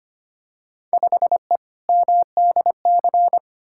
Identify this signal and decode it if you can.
Morse code, heard as a keyed tone, about 25 wpm, 700 Hz. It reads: 5E MDC
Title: Field Day Exchanges - 25wpm